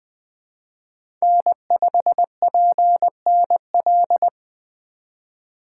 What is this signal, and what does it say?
Morse code, heard as a keyed tone, about 20 wpm, 700 Hz. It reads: N5PNL
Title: USA Call Signs - 2x Character Spacing - 20wpm